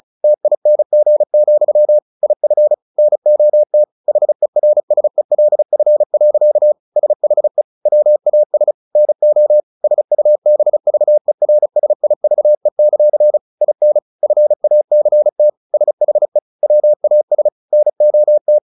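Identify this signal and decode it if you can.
Morse code, heard as a keyed tone, about 34 wpm, 600 Hz. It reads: TING, IF NOT HERSELF. SHE WAS NO SUBVERSIVE; IN FACT SHE WAS NOT
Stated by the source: Morse practice at 35 wpm from CW med Gnister, sending Ken Follett's novel Never in English.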